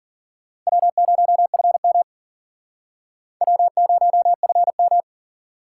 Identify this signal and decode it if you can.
Morse code, heard as a keyed tone, about 52 wpm, 700 Hz. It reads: W0FM W0FM